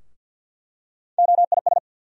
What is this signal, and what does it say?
Morse code, heard as a keyed tone, about 53 wpm, 700 Hz. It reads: OIS
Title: USA Call Sign Postfixes - 3 Letters 50wpm